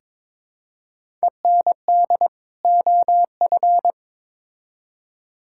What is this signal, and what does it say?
Morse code, heard as a keyed tone, about 21 wpm, 700 Hz. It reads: END OF
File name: Build A Phrase - 2-Word Phrases - 22wpm